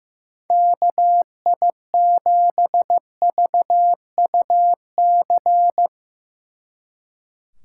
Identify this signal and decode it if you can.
Morse code, heard as a keyed tone, about 17 wpm, 700 Hz. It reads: KI7VUC
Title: Speed Racing - USA 2x3 Call Signs 15wpm